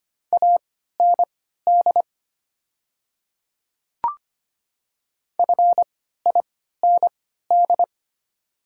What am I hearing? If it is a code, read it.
Morse code, heard as a keyed tone, about 25 wpm, spaced as for 14 wpm, 700 Hz. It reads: AND FIND
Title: 3x Character Spacing - Top 200 Words - 25wpm